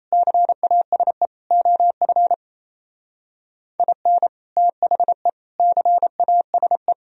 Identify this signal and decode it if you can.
Morse code, heard as a keyed tone, about 33 wpm, 700 Hz. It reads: CASE OF IN THE CASE
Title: Speed-Racing - Sets of 4 Words 28wpm